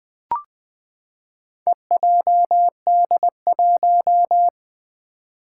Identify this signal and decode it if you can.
Morse code, heard as a keyed tone, about 20 wpm, 700 Hz. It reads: EJD1
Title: Four Challenging Letter-Number Combinations 20wpm